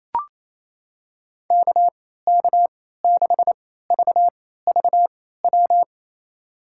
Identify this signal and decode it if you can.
Morse code, heard as a keyed tone, about 28 wpm, spaced as for 16 wpm, 700 Hz. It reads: KK6VVW
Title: USA Call Signs - 3x Character Spacing - 28wpm